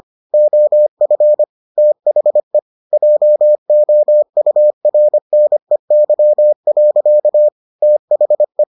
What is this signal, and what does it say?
Morse code, heard as a keyed tone, about 25 wpm, 600 Hz. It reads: OF THE JOURNEY. THE